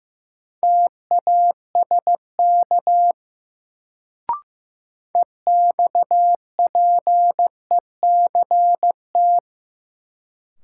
Morse code, heard as a keyed tone, about 15 wpm, 700 Hz. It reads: TASK EXPECT
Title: Sentences from Top 1100 Words - New Word Drill Alt - 15wpm